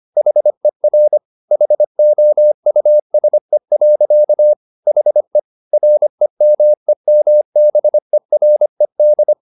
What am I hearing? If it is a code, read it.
Morse code, heard as a keyed tone, about 25 wpm, 600 Hz. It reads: HER HOUSE. HE REMEMBERED